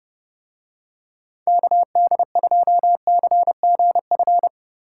Morse code, heard as a keyed tone, about 28 wpm, 700 Hz. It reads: KD2CGF